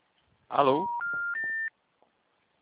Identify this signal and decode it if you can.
Special information tone, with speech.
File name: out- -2002- - - .50355